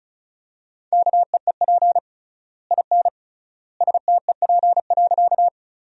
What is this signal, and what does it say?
Morse code, heard as a keyed tone, about 36 wpm, 700 Hz. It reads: KEEP IN STEP.
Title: 3x Word-Spacing - Sentences from Top 400 Words - 35wpm